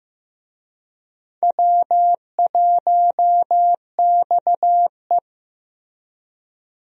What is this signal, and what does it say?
Morse code, heard as a keyed tone, about 15 wpm, 700 Hz. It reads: W1XE
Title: USA Call Signs - 4 Characters - Repeated 3x - 15wpm